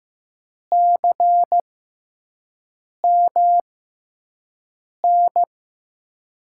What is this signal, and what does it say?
Morse code, heard as a keyed tone, about 16 wpm, spaced as for 5 wpm, 700 Hz. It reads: CMN
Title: USA Call Signs - 6 Characters - 6x Character Spacing - 15wpm